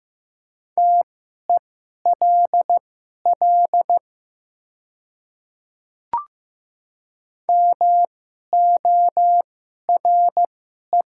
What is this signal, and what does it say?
Morse code, heard as a keyed tone, about 15 wpm, spaced as for 11 wpm, 700 Hz. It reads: TELL MORE
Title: 2x Character Spacing - Top 200 Words - 15wpm